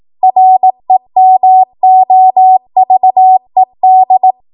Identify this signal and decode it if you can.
Morse code, heard as a keyed tone, about 18 wpm, 750 Hz. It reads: REMOVED